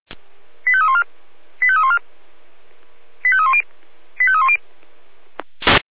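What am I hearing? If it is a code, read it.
Five-tone selective call (ZVEI), heard as quick runs of five tones, tones of about 70 ms. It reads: 85314 85314 85310 85310
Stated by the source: Received on an SDR, ZVEI five-tone selective call.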